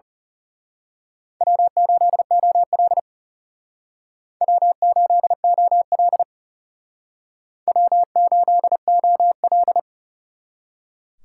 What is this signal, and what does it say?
Morse code, heard as a keyed tone, about 35 wpm, 700 Hz. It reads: W8OL W8OL W8OL
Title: Speed Racing - USA Call Signs 30wpm